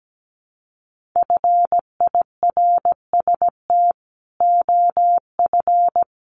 Morse code, heard as a keyed tone, about 17 wpm, 700 Hz. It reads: FIRST OF
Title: Sentences from Top 1500 Words 17wpm